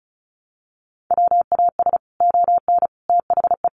Morse code, heard as a keyed tone, about 34 wpm, 700 Hz. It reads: WAS ON THE
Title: Sets of 5 Words Encore 35wpm